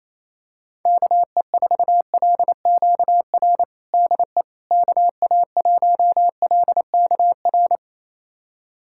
Morse code, heard as a keyed tone, about 28 wpm, 700 Hz. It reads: KE4LQR DE KA1LKR
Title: USA Callsign DE Callsign - 6 Characters - 28wpm